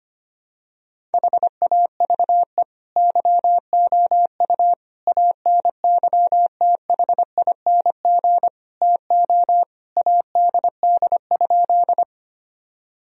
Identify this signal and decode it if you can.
Morse code, heard as a keyed tone, about 25 wpm, 700 Hz. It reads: HAVE YOU ANYTHING TO ADD?
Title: Sentences from Top 1000 Words - Review Part 1 - 25wpm